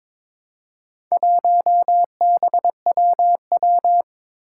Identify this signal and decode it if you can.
Morse code, heard as a keyed tone, about 23 wpm, 700 Hz. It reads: 1BWW
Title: Four Challenging Letter-Number Combinations 22wpm